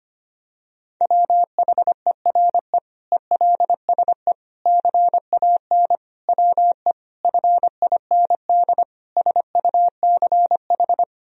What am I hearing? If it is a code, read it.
Morse code, heard as a keyed tone, about 25 wpm, 700 Hz. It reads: WHERE ELSE CAN WE FIND SUCH
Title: Sentences from Top 1000 Words - Review Part 1 - 25wpm